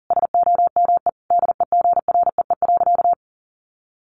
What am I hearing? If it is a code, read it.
Morse code, heard as a keyed tone, about 40 wpm, 700 Hz. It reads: SOME DEGREE.
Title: Sentences from Top 1500 Words 40wpm